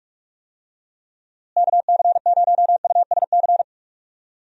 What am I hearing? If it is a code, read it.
Morse code, heard as a keyed tone, about 42 wpm, 700 Hz. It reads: KC0USC
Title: USA Call Signs - 6 Characters - Repeated 3x - 45wpm